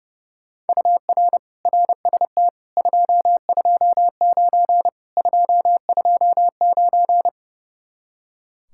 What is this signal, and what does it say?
Morse code, heard as a keyed tone, about 40 wpm, 700 Hz. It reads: UR RST 229 229